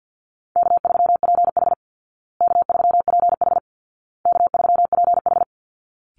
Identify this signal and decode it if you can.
Morse code, heard as a keyed tone, about 48 wpm, 700 Hz. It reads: K3PH K3PH K3PH
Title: Call Signs Repeated 2x - 50wpm